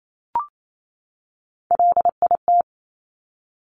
Morse code, heard as a keyed tone, about 28 wpm, 700 Hz. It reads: LIT
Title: Top Three Letter Combinations in Words 28wpm